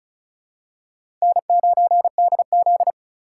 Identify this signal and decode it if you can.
Morse code, heard as a keyed tone, about 34 wpm, 700 Hz. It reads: N9DZ
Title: USA Call Signs - 4 Characters - 4x Character Spacing - 35wpm